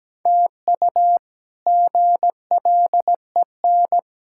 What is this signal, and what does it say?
Morse code, heard as a keyed tone, about 17 wpm, 700 Hz. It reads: TU GLEN
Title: K1USN SST QSO Practice - 17wpm